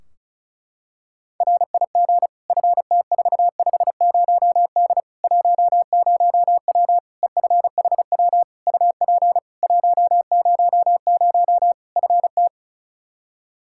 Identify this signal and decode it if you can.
Morse code, heard as a keyed tone, about 36 wpm, 700 Hz. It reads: RIG FT450D 10W EFHW UP 100 FT
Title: K1USN SST QSO Practice 35wpm